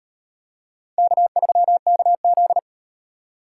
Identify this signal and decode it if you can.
Morse code, heard as a keyed tone, about 36 wpm, 700 Hz. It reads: K3KZ